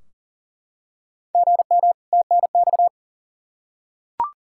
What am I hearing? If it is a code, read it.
Morse code, heard as a keyed tone, about 40 wpm, 700 Hz. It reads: GM TNX
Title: Practice QSO - Exchange 1 - Beginning - 40wpm